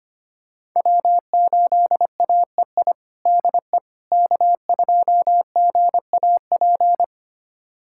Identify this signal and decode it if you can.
Morse code, heard as a keyed tone, about 25 wpm, 700 Hz. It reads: W8AEI DE K2GAP
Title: USA Callsign DE Callsign - 5 Characters - 25wpm